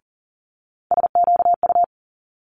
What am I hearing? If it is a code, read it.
Morse code, heard as a keyed tone, about 38 wpm, 700 Hz. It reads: SQU
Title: Multi-letter Phonemes Quick Fire 40wpm